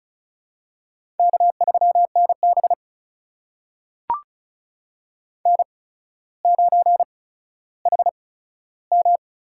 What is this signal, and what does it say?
Morse code, heard as a keyed tone, about 36 wpm, spaced as for 15 wpm, 700 Hz. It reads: K3NB N9HM
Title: USA Call Signs - 4 Characters - 8x Character Spacing - 35wpm